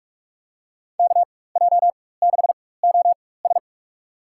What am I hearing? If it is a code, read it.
Morse code, heard as a keyed tone, about 46 wpm, spaced as for 21 wpm, 700 Hz. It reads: KJ6OS
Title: USA Call Signs - 5 Characters - 4x Character Spacing - 45wpm